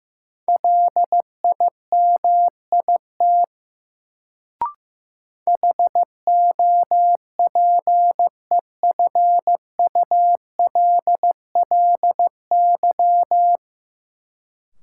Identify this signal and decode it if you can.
Morse code, heard as a keyed tone, about 15 wpm, 700 Hz. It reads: LIMIT HOPEFULLY
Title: Sentences from Top 1500 Words - New Word Drill - 15wpm